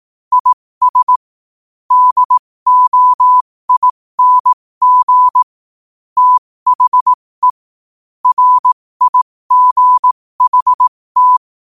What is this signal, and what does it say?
Morse code, heard as a keyed tone, about 18 wpm, spaced as for 15 wpm, 1000 Hz. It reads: IS DOING THE RIGHT